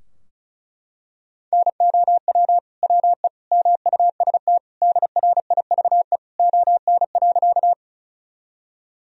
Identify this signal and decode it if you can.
Morse code, heard as a keyed tone, about 34 wpm, 700 Hz. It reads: NOW WE MUST DRIVE ON.
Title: Sentences from Top 1000 Words - Review Part 2 - 35wpm